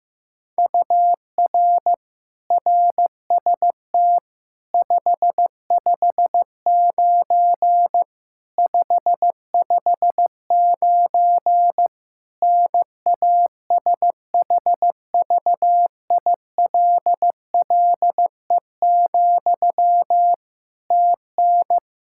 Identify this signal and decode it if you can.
Morse code, heard as a keyed tone, about 15 wpm, 700 Hz. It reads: UR RST 559 559 NASHVILLE, TN
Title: Practice QSO - Exchange 1 - Beginning - 15wpm